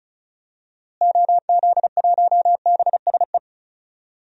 Morse code, heard as a keyed tone, about 36 wpm, 700 Hz. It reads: OZ1BSE
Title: DXCC-100 Call Signs - 6 Characters - Repeated 3x - 35wpm